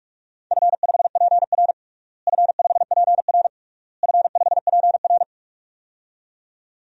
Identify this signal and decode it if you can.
Morse code, heard as a keyed tone, about 46 wpm, 700 Hz. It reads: F5PR F5PR F5PR
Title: DXCC-100 Call Signs - 4 Characters - Repeated 3x - 45wpm